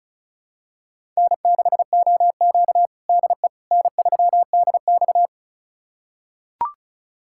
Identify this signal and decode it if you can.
Morse code, heard as a keyed tone, about 36 wpm, 700 Hz. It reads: N6OQ DE N3DX
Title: USA Callsign DE Callsign - 4 Characters - 35wpm